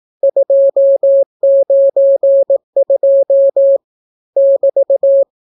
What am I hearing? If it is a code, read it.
Morse code, heard as a keyed tone, about 18 wpm, 550 Hz. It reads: 292 <BT>